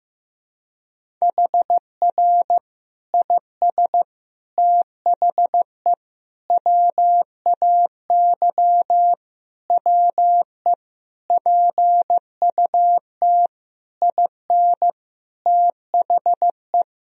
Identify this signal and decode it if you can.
Morse code, heard as a keyed tone, about 15 wpm, 700 Hz. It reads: HR IS THE WAY WE PUT IN THE
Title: Sentences from Top 200 Words and Abbreviations 15wpm